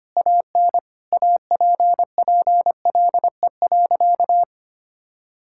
Morse code, heard as a keyed tone, about 25 wpm, 700 Hz. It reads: AN APPLE.
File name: practice / 25wpm / Sentences with 2-Word Phrases - 25wpm